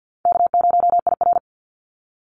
Morse code, heard as a keyed tone, about 50 wpm, 700 Hz. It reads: K0IR
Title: Call Signs - 50wpm